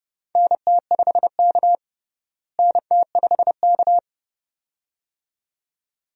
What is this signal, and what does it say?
Morse code, heard as a keyed tone, about 32 wpm, 700 Hz. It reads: NT5K NT5K
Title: USA Call Signs - 4 Characters - Repeated 2x - 30wpm